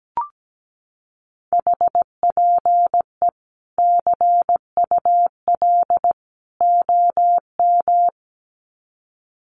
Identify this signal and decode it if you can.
Morse code, heard as a keyed tone, about 17 wpm, 700 Hz. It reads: HPE CUL OM
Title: Common QSO Phrases 17wpm